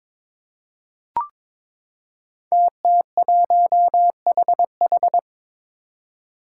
Morse code, heard as a keyed tone, about 23 wpm, 700 Hz. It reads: TT1HH
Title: Five Challenging Letter-Number Combinations 22wpm